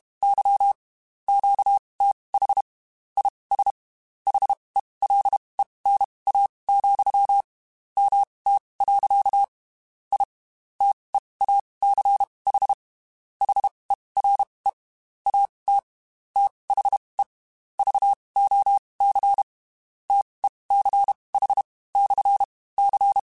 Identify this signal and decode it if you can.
Morse code, heard as a keyed tone, about 32 wpm, spaced as for 23 wpm, 800 Hz. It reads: Y QTH IS HELENA, MT. I TEACH HERE AT THE VOC TECH/C
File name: QSOs / QSOs-mp3s / WD0FBT 1@30